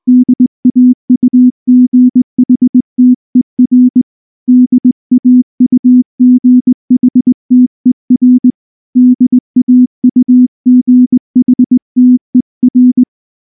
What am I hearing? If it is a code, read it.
Morse code, heard as a keyed tone, about 19 wpm, 250 Hz. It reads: DAUGHTER DAUGHTER DAUGHTER
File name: morse / morse-training-20WPM-6